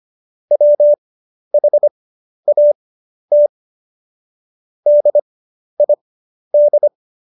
Morse code, heard as a keyed tone, about 25 wpm, spaced as for 11 wpm, 590 Hz. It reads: WHAT DID
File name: Jokes-25-char-10-eff